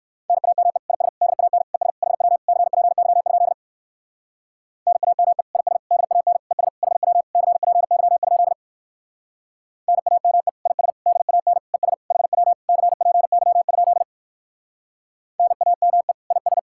Morse code, heard as a keyed tone, about 58 wpm, 700 Hz. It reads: NAME IS DAN ES HW CPY? NAME IS DAN ES HW CPY? NAME IS DAN ES HW CPY? NAME IS